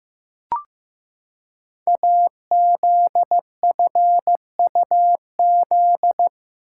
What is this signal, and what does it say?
Morse code, heard as a keyed tone, about 15 wpm, 700 Hz. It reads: AZFUZ
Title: Code Groups - 5 Letters - 15wpm